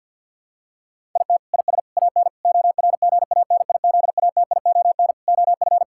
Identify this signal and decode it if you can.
Morse code, heard as a keyed tone, about 50 wpm, 700 Hz. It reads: IT IS AN ORGANIZATION OF
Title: Sentences from Top 1000 Words - Review Part 6 - 50wpm